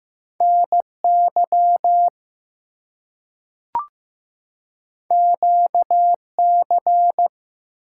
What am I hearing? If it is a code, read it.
Morse code, heard as a keyed tone, about 15 wpm, 700 Hz. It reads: NY QC
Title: US States and Canadian Provinces 15wpm